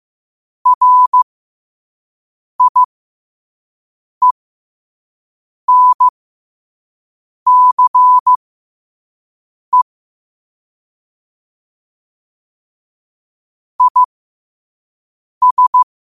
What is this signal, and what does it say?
Morse code, heard as a keyed tone, about 15 wpm, spaced as for 5 wpm, 1000 Hz. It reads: RIENCE IS
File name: qotdfiles / 5wpm